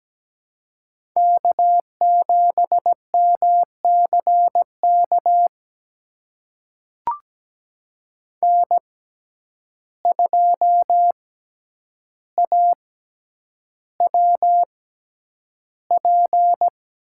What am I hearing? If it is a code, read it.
Morse code, heard as a keyed tone, about 17 wpm, spaced as for 9 wpm, 700 Hz. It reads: K7MCK N2AWP
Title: USA Call Signs - 5 Characters - 6x Character Spacing - 17wpm